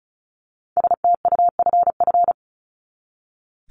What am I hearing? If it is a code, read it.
Morse code, heard as a keyed tone, about 34 wpm, 700 Hz. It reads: STUFF
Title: Top 1251-1500 Words 35wpm